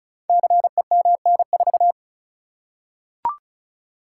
Morse code, heard as a keyed tone, about 36 wpm, 690 Hz. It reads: CEMN4
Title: Code Groups - 5 Letter:Numbers - 35wpm